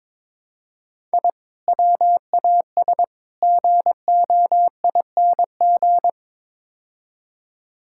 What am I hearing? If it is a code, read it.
Morse code, heard as a keyed tone, about 22 wpm, 700 Hz. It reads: I WAS GOING
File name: practice / 22wpm / Build A Phrase - 5-Word Phrases - 22wpm